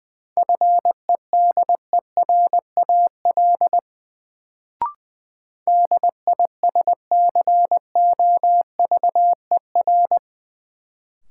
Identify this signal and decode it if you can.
Morse code, heard as a keyed tone, about 20 wpm, 700 Hz. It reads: FEDERAL DISCOVER
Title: Sentences from Top 1100 Words - New Word Drill Alt - 20wpm